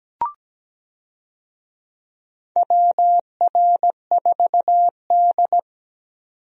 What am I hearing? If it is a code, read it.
Morse code, heard as a keyed tone, about 17 wpm, 700 Hz. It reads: WR4D
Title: USA Call Signs - 4 Characters - 17wpm